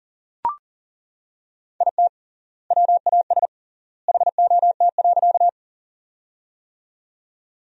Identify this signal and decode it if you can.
Morse code, heard as a keyed tone, about 40 wpm, 700 Hz. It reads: IT WAS HOT.
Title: 3x Word-Spacing - Sentences from Top 500 Words - 40wpm